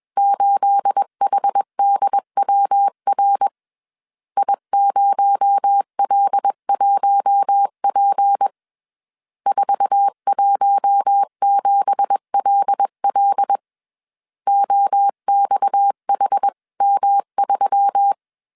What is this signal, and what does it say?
Morse code, heard as a keyed tone, about 21 wpm, 800 Hz. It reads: 8HDWR I0L1P 417LL OXHM3